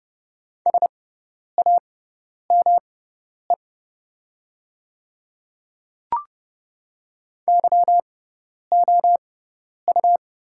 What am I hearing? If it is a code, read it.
Morse code, heard as a keyed tone, about 29 wpm, spaced as for 10 wpm, 690 Hz. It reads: SAME YOU
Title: 6x Character Spacing - Top 200 Words - 30wpm